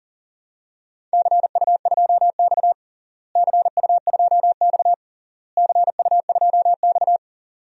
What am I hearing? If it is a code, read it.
Morse code, heard as a keyed tone, about 40 wpm, 700 Hz. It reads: CU2X CU2X CU2X
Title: DXCC-100 Call Signs - 4 Characters - Repeated 3x - 40wpm